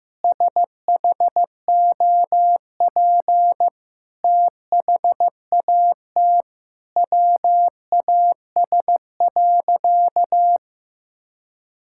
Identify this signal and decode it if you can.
Morse code, heard as a keyed tone, about 15 wpm, 700 Hz. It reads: SHOP THAT WAS.
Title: Sentences from Top 1600 Words 15wpm